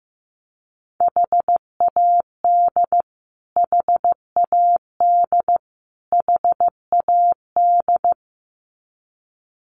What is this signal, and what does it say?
Morse code, heard as a keyed tone, about 15 wpm, 700 Hz. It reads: HAD HAD HAD
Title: Top 500 Words Repeated 15wpm